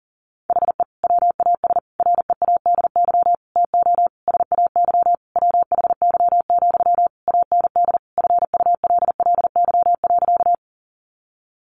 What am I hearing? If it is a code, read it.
Morse code, heard as a keyed tone, about 40 wpm, 700 Hz. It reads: HE WAS READY TO SAY WHY, AND FULLY.
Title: Sentences from Top 1300 Words 40wpm